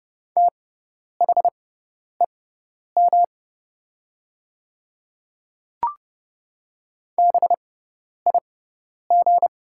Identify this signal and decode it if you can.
Morse code, heard as a keyed tone, about 30 wpm, spaced as for 10 wpm, 700 Hz. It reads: THEM BIG
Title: 6x Character Spacing - Top 300 Words - 30wpm